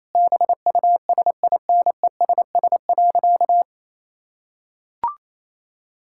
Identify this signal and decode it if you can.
Morse code, heard as a keyed tone, about 28 wpm, 700 Hz. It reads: BUSINESS.
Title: Sentences from Top 1000 Words - Review Part 6 - 28wpm